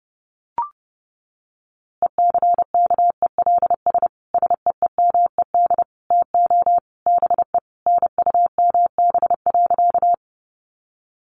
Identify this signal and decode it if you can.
Morse code, heard as a keyed tone, about 30 wpm, 700 Hz. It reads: ECKELS SEEMED TO BE NUMB.
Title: A Sound Of Thunder 30wpm